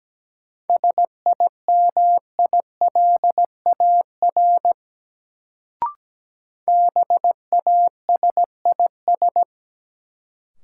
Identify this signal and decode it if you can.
Morse code, heard as a keyed tone, about 17 wpm, 700 Hz. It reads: SIMILAR BASIS
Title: Sentences from Top 1000 Words - Review Part 5 - New Word Drill - 17wpm